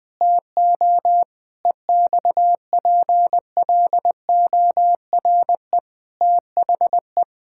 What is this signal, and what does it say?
Morse code, heard as a keyed tone, about 20 wpm, 700 Hz. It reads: TO EXPLORE THE
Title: Sentences with 5-Word Phrases - Spoken Beginning - 20wpm